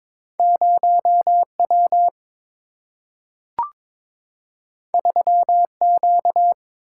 Morse code, heard as a keyed tone, about 22 wpm, 690 Hz. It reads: 0W 3Q
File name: practice / 22wpm / Two Letter-Number Combinations - 22wpm